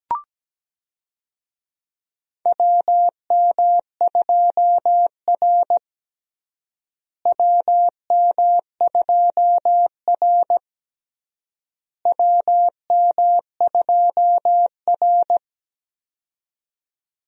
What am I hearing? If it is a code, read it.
Morse code, heard as a keyed tone, about 17 wpm, 700 Hz. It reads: WM2R WM2R WM2R
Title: USA Call Signs - 4 Characters - Repeated 3x - 17wpm